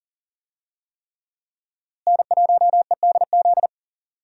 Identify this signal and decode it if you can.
Morse code, heard as a keyed tone, about 40 wpm, 700 Hz. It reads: N1EDZ
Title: USA Call Signs - 5 Characters - Repeated 3x - 40wpm